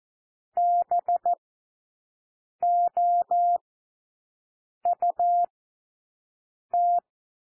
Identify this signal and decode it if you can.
Morse code, heard as a keyed tone, about 14 wpm, spaced as for 6 wpm, 700 Hz. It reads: BOUT
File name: cwmp3 / W1 SOUND WORDS 14 wpm 0